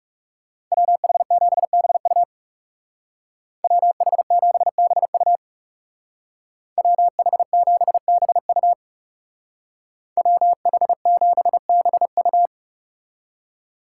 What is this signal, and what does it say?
Morse code, heard as a keyed tone, about 39 wpm, 690 Hz. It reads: WH7BU WH7BU WH7BU WH7BU